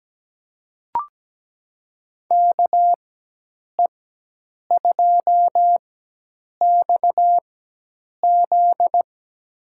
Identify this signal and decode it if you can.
Morse code, heard as a keyed tone, about 17 wpm, spaced as for 8 wpm, 700 Hz. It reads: KE2XZ
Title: USA Call Signs - 4x Character Spacing - 17wpm